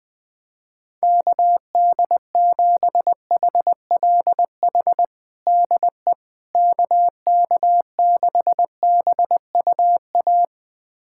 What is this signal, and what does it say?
Morse code, heard as a keyed tone, about 20 wpm, 700 Hz. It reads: KD7HLH DE KK6BUA
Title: USA Callsign DE Callsign - 6 Characters - 20wpm